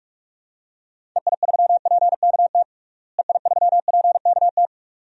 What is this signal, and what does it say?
Morse code, heard as a keyed tone, about 46 wpm, 700 Hz. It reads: EI3PKT EI3PKT
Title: DXCC-100 Call Signs - 6 Characters - Repeated 2x - 45wpm